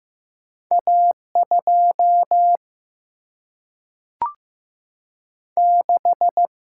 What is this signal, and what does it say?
Morse code, heard as a keyed tone, about 15 wpm, 700 Hz. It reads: A2 6
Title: Single-Letter Single-Number 15wpm